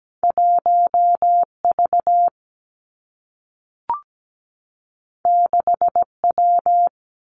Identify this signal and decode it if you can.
Morse code, heard as a keyed tone, about 17 wpm, 700 Hz. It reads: 1V 6W
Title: Two Challenging Letter-Number Combinations 17wpm